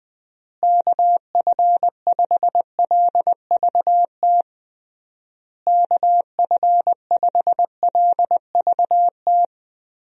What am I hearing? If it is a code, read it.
Morse code, heard as a keyed tone, about 20 wpm, 700 Hz. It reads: KF5LVT KF5LVT
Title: USA Call Signs - 6 Characters - Repeated 3x - 20wpm